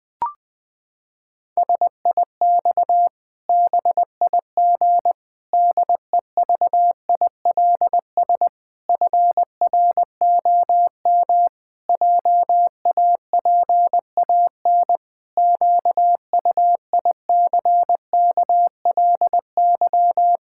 Morse code, heard as a keyed tone, about 20 wpm, 700 Hz. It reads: SIX BIG DEVILS FROM JAPAN QUICKLY
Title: Pangram Challenge v2 - 20wpm